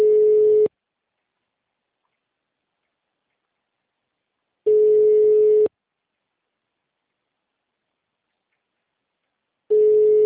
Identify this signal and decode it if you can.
Ringback tone (Europe, 425 Hz).
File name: out- -2008- - - .2353